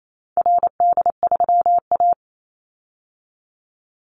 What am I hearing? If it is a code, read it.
Morse code, heard as a keyed tone, about 28 wpm, 700 Hz. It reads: RD3A
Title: Call Signs Repeated 2x - 28wpm